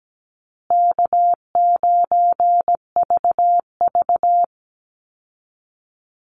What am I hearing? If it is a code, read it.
Morse code, heard as a keyed tone, about 17 wpm, 700 Hz. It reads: K9VV
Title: Easy Call Signs Repeated 3x - 17wpm